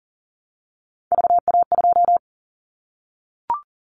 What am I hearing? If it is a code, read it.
Morse code, heard as a keyed tone, about 40 wpm, 700 Hz. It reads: VA2
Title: Callsign Prefixes for Top 50 DXCC - 40wpm